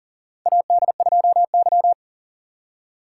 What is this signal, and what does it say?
Morse code, heard as a keyed tone, about 40 wpm, 700 Hz. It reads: AD2Y
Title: USA Call Signs - 4 Characters - 3x Character Spacing - 40wpm